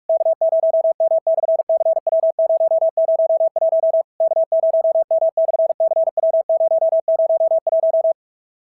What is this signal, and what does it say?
Morse code, heard as a keyed tone, about 46 wpm, 650 Hz. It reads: K0M/CW001 K0M/CW001
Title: SOTA QSO Practice 45wpm